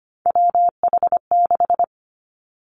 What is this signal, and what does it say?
Morse code, heard as a keyed tone, about 25 wpm, 700 Hz. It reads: WH6
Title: Three Challenging Letter-Number Combinations 25wpm